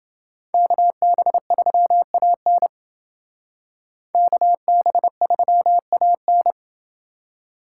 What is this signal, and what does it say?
Morse code, heard as a keyed tone, about 28 wpm, 700 Hz. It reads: KB3AN KB3AN